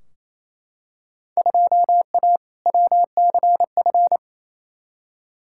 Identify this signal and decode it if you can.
Morse code, heard as a keyed tone, about 28 wpm, 700 Hz. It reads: 2A WCF
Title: Field Day Exchanges Encore - 28wpm